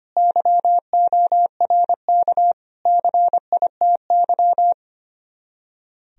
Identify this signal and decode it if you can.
Morse code, heard as a keyed tone, about 30 wpm, 700 Hz. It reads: YORK CITY
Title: Speed-Racing - Sets of 4 Words 25wpm